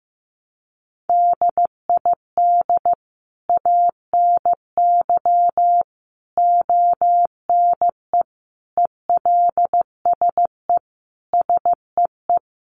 Morse code, heard as a keyed tone, about 14 wpm, 700 Hz. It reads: DID ANY ONE ELSE SEE
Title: Sentences from Top 1000 Words 15wpm